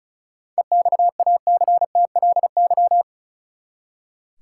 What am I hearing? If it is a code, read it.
Morse code, heard as a keyed tone, about 36 wpm, 700 Hz. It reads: EXACTLY
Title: Seven Letter Words - 35wpm